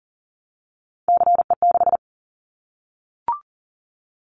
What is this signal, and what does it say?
Morse code, heard as a keyed tone, about 40 wpm, 700 Hz. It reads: CE6
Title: Callsign Prefixes for Top 100 DXCC - 40wpm